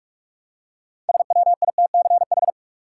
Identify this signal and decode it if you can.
Morse code, heard as a keyed tone, about 47 wpm, 700 Hz. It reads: SWITCH
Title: Six Letter Words - 45wpm